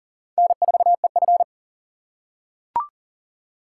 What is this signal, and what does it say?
Morse code, heard as a keyed tone, about 38 wpm, 700 Hz. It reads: N4EF
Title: USA Call Signs - 4 Characters - 4x Character Spacing - 40wpm